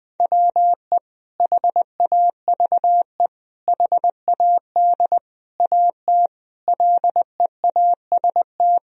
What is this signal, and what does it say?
Morse code, heard as a keyed tone, about 20 wpm, 700 Hz. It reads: WE HAVE HAD AT LEAST
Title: Sentences from Top 1000 Words - Review Part 1 - 20wpm